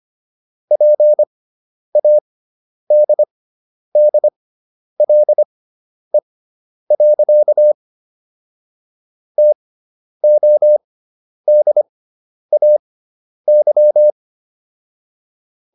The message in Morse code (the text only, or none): PADDLE. TODAY